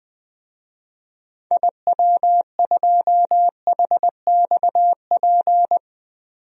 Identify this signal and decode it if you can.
Morse code, heard as a keyed tone, about 21 wpm, 700 Hz. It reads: IW2HXP
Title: DXCC-100 Call Signs - 6 Characters - 20wpm